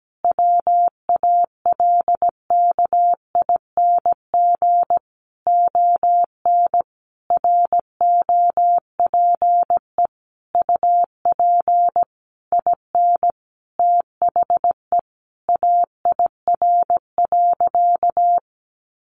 Morse code, heard as a keyed tone, about 17 wpm, 700 Hz. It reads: WALKING ON ROPE UP IN THE AIR.
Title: Easy Phrases 17wpm